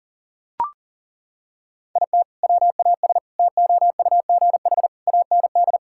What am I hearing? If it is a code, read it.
Morse code, heard as a keyed tone, about 37 wpm, 700 Hz. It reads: IT WAS TOUGH AND
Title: Sentences from Top 1600 Words 40wpm